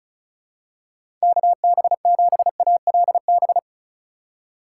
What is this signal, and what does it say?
Morse code, heard as a keyed tone, about 36 wpm, 700 Hz. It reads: KB7ALB
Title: USA Call Signs - 6 Characters - 3x Character Spacing - 35wpm